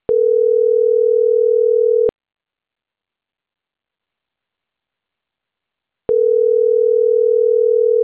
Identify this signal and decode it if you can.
Ringback tone (North America), 440 + 480 Hz.